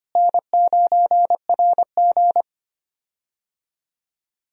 Morse code, heard as a keyed tone, about 25 wpm, 700 Hz. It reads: N9RG
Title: USA Call Signs - 4 Characters - Repeated 3x - 25wpm